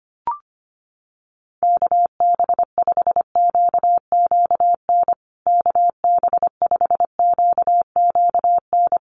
Morse code, heard as a keyed tone, about 25 wpm, 700 Hz. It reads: KB5QQN KB5QQN
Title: USA Call Signs Repeated 3x 25wpm